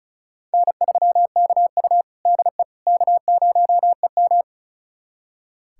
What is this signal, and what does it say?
Morse code, heard as a keyed tone, about 36 wpm, 700 Hz. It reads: N3KU DE K0EM
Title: USA Callsign DE Callsign - 4 Characters - 35wpm